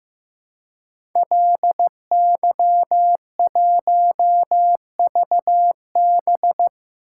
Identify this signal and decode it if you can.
Morse code, heard as a keyed tone, about 15 wpm, 700 Hz. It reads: LY1VB
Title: Code Groups - 5 Letter:Numbers - 15wpm